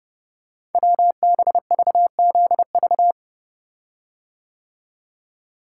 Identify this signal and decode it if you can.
Morse code, heard as a keyed tone, about 30 wpm, 700 Hz. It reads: WBVZV
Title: Code Groups - 5 Letters - 30wpm